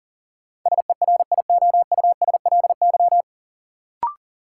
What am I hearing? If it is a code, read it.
Morse code, heard as a keyed tone, about 40 wpm, 700 Hz. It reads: SERIOUSLY
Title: Sentences from Top 1200 Words - New Word Drill Alt - 40wpm